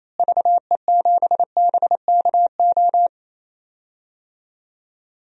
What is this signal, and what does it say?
Morse code, heard as a keyed tone, about 28 wpm, 700 Hz. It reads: VE7BKO
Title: DXCC-100 Call Signs - 6 Characters - Repeated 3x - 28wpm